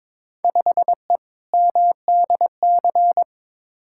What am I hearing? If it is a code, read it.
Morse code, heard as a keyed tone, about 22 wpm, 700 Hz. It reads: 5E MDC
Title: Field Day Exchanges - 22wpm